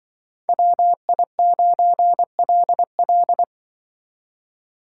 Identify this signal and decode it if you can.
Morse code, heard as a keyed tone, about 24 wpm, 700 Hz. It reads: WI9LL
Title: Speed Racing - USA Call Signs 20wpm